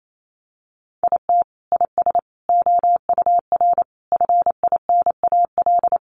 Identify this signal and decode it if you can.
Morse code, heard as a keyed tone, about 28 wpm, 700 Hz. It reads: IT IS OUR FINAL
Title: Sentences from Top 1200 Words 28wpm